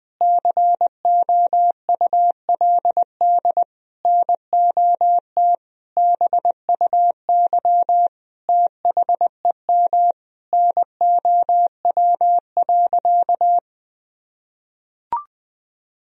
Morse code, heard as a keyed tone, about 21 wpm, 700 Hz. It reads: COULD NOT BUY THEM NOW.
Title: Sentences from Top 1000 Words - Review Part 4 - 20wpm